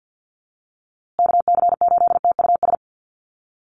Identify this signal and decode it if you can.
Morse code, heard as a keyed tone, about 50 wpm, 700 Hz. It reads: KC8TUS
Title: USA Call Signs Repeated 2x 50wpm